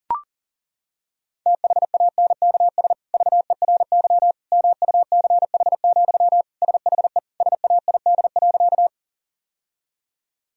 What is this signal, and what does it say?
Morse code, heard as a keyed tone, about 37 wpm, 700 Hz. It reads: THANKS VERY MUCH, SHE SAID.
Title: Sentences from Top 1000 Words - Review Part 3 - 40wpm